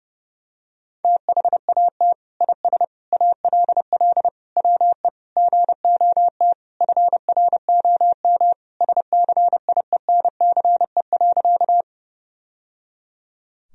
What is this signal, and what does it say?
Morse code, heard as a keyed tone, about 30 wpm, 700 Hz. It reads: THAT IS ALL WE GOT FROM SCIENCE.
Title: Sentences from Top 1000 Words - Review Part 2 - 30wpm